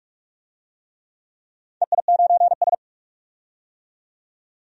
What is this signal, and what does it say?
Morse code, heard as a keyed tone, about 44 wpm, 690 Hz. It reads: EI9S
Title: DXCC-100 Call Signs - 4 Characters - 45wpm